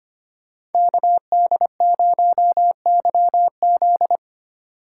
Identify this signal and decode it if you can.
Morse code, heard as a keyed tone, about 32 wpm, 700 Hz. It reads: KD0YZ